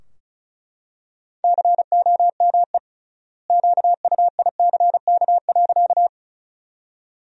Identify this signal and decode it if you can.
Morse code, heard as a keyed tone, about 36 wpm, 700 Hz. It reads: COME QUICK.
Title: 3x Word-Spacing - Sentences from Top 500 Words - 35wpm